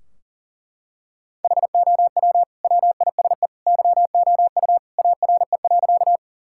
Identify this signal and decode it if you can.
Morse code, heard as a keyed tone, about 40 wpm, 700 Hz. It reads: HOW WISE YOU ARE.
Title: Sentences from Top 1600 Words 40wpm